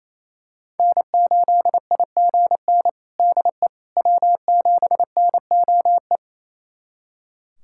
Morse code, heard as a keyed tone, about 28 wpm, 700 Hz. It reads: N8IGN DE W7NOE
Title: USA Callsign DE Callsign - 5 Characters - 28wpm